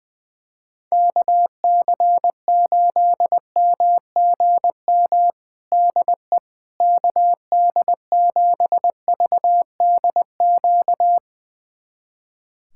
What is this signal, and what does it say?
Morse code, heard as a keyed tone, about 20 wpm, 700 Hz. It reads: KC8MGM DE KD7VDQ
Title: USA Callsign DE Callsign - 6 Characters - 20wpm